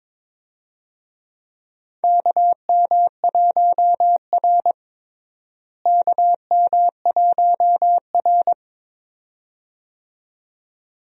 Morse code, heard as a keyed tone, about 22 wpm, 700 Hz. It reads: KM1R KM1R
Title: USA Call Signs - 4 Characters - Repeated 2x - 22wpm